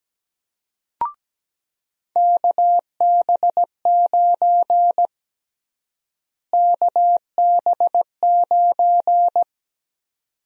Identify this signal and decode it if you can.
Morse code, heard as a keyed tone, about 17 wpm, 700 Hz. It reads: KB9 KB9
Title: USA Call Sign Prefixes - Repeated 2x - 17wpm